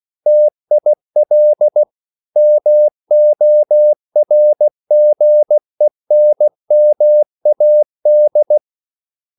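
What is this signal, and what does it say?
Morse code, heard as a keyed tone, about 16 wpm, 600 Hz. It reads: TIL MORGENMAD
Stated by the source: Morse code at a slow 15 wpm from CW med Gnister.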